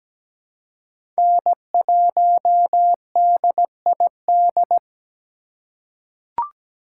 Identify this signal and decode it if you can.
Morse code, heard as a keyed tone, about 17 wpm, 700 Hz. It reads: N1DID
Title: K1USN SST QSO Practice - 17wpm